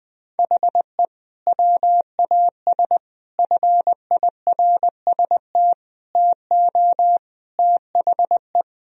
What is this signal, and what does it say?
Morse code, heard as a keyed tone, about 20 wpm, 700 Hz. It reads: HE WAS FIRST TO THE
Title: Sentences from Top 1000 Words - Review Part 2 - 20wpm